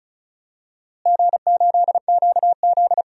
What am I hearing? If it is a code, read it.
Morse code, heard as a keyed tone, about 34 wpm, 700 Hz. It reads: G8QZ
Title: DXCC-100 Call Signs - 4 Characters - Repeated 3x - 35wpm